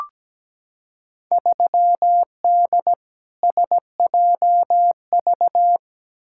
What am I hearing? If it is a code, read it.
Morse code, heard as a keyed tone, about 17 wpm, 700 Hz. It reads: 3D SJV
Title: Field Day Exchanges - 17wpm